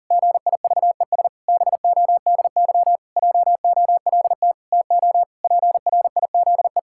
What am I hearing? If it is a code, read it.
Morse code, heard as a keyed tone, about 37 wpm, 700 Hz. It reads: GIVES BODY JOLT TO PRIZE
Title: Pangram Challenge v2 - 40wpm